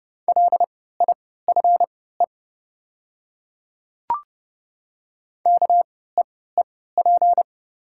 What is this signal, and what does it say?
Morse code, heard as a keyed tone, about 30 wpm, spaced as for 17 wpm, 700 Hz. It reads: LIFE KEEP
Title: 3x Character Spacing - Top 300 Words - 30wpm